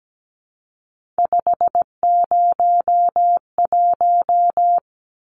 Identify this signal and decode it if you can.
Morse code, heard as a keyed tone, about 17 wpm, 700 Hz. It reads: 501
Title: Numbers - 3-Digit 17wpm